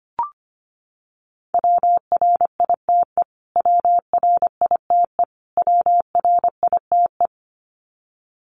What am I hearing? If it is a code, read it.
Morse code, heard as a keyed tone, about 25 wpm, 700 Hz. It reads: WRITE WRITE WRITE
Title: Top 200 Words Repeated 25wpm